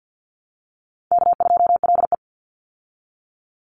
Morse code, heard as a keyed tone, about 50 wpm, 700 Hz. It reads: K2LE
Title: Easy Call Signs Repeated 3x - 50wpm